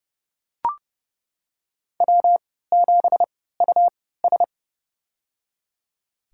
Morse code, heard as a keyed tone, about 30 wpm, spaced as for 17 wpm, 700 Hz. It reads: W7US
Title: USA Call Signs - 4 Characters - 3x Character Spacing - 30wpm